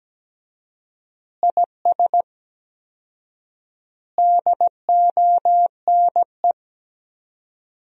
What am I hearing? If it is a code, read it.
Morse code, heard as a keyed tone, about 17 wpm, 700 Hz. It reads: IS DONE
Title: 4x Word-Spacing - Sentences from Top 500 Words - 17wpm